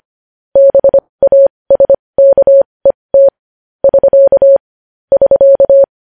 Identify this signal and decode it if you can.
Morse code, heard as a keyed tone, about 25 wpm, spaced as for 20 wpm, 550 Hz. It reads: BASKET <SK> <SK>
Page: news-headlines-in-morse-code-at-20-wpm